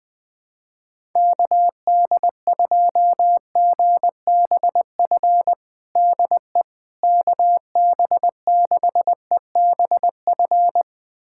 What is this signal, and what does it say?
Morse code, heard as a keyed tone, about 20 wpm, 700 Hz. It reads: KD2GBF DE KB6EBF
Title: USA Callsign DE Callsign 20wpm